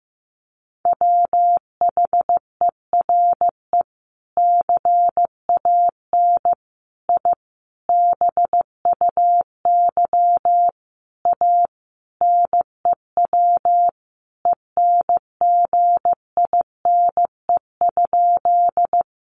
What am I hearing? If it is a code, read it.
Morse code, heard as a keyed tone, about 15 wpm, 700 Hz. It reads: WHERE CAN I BUY A NEW ENGINE?
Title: Sentences from Top 1500 Words 15wpm